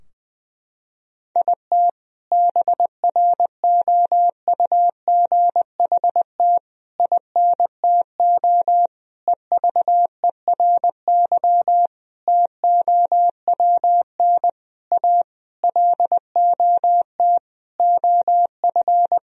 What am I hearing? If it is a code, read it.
Morse code, heard as a keyed tone, about 20 wpm, 700 Hz. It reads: IT BROUGHT INTO EVERY TOWN A LOT OF
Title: Sentences from Top 1000 Words - Review Part 3 - 20wpm